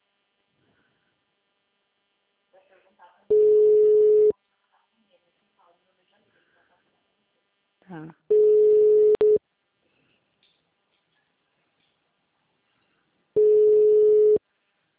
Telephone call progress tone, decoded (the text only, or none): ringback tone